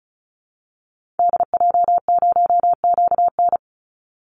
Morse code, heard as a keyed tone, about 34 wpm, 700 Hz. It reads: DJ0QN
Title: Call Signs Repeated 3x - 35wpm